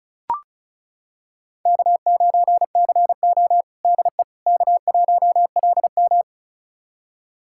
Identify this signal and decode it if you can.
Morse code, heard as a keyed tone, about 36 wpm, 700 Hz. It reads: K9CO DE K1LM
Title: USA Callsign DE Callsign - 4 Characters - 35wpm